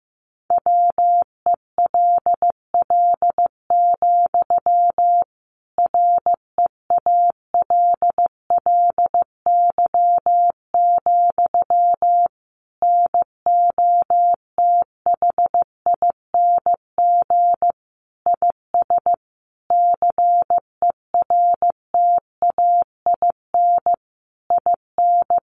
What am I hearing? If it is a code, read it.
Morse code, heard as a keyed tone, about 15 wpm, 700 Hz. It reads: WELL, REALLY, NOTHING IS CERTAIN IN
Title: Sentences from Top 600 Words 15wpm